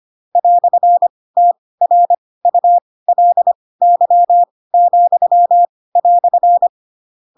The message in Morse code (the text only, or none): "TRULY,"